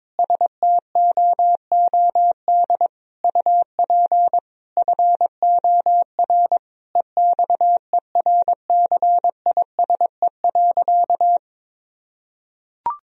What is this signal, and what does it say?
Morse code, heard as a keyed tone, about 22 wpm, 700 Hz. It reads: STOOD UP FOR EXERCISE.
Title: Sentences from Top 1000 Words - Review Part 4 - 22wpm